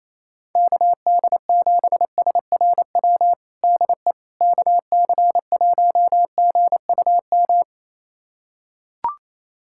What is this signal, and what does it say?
Morse code, heard as a keyed tone, about 28 wpm, 700 Hz. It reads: KD7SRW DE KC1GUM
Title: USA Callsign DE Callsign - 6 Characters - 28wpm